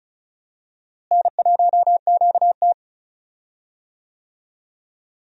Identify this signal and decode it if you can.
Morse code, heard as a keyed tone, about 36 wpm, 700 Hz. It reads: N1QT
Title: USA Call Signs - 4 Characters - Repeated 2x - 35wpm